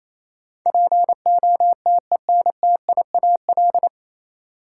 Morse code, heard as a keyed tone, about 28 wpm, 700 Hz. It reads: POTENTIAL
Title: Sentences from Top 1000 Words - Review Part 4 - New Word Drill - 28wpm